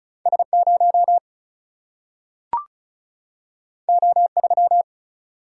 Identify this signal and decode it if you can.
Morse code, heard as a keyed tone, about 36 wpm, 700 Hz. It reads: S0 O3
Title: Single-Letter Single-Number 35wpm